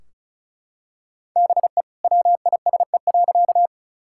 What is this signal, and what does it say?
Morse code, heard as a keyed tone, about 36 wpm, 700 Hz. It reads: BE WISE.
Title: Sentences from Top 1600 Words 35wpm